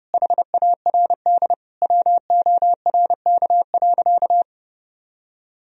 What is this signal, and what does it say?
Morse code, heard as a keyed tone, about 30 wpm, 700 Hz. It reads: HARD WORK.
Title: Sentences from Top 1000 Words - Review Part 4 - 30wpm